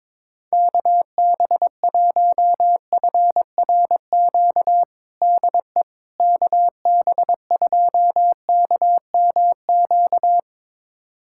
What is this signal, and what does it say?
Morse code, heard as a keyed tone, about 23 wpm, 700 Hz. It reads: KB1FRQ DE KB2KMQ
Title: USA Callsign DE Callsign - 6 Characters - 22wpm